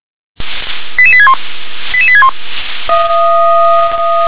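Five-tone selective call (ZVEI), heard as quick runs of five tones, tones of about 70 ms. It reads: 88861 88861
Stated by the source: ZVEI five-tone selective call 88100, received by SDR.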